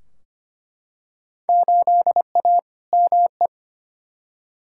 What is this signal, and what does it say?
Morse code, heard as a keyed tone, about 25 wpm, 700 Hz. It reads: 8A ME
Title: Field Day Exchanges - 25wpm